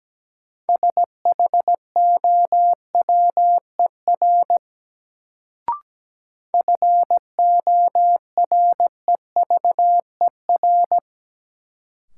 Morse code, heard as a keyed tone, about 17 wpm, 700 Hz. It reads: SHOWER FOREVER
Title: Sentences from Top 1600 Words - New Word Drill - 17wpm